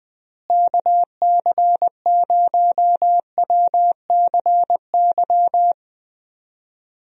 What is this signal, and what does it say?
Morse code, heard as a keyed tone, about 21 wpm, 700 Hz. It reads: KC0WCY
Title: USA Call Signs - 6 Characters - Repeated 3x - 20wpm